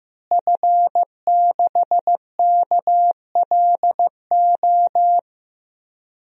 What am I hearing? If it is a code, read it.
Morse code, heard as a keyed tone, about 15 wpm, 700 Hz. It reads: F6KLO
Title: DXCC-100 Call Signs - 5 Characters - Repeated 3x - 15wpm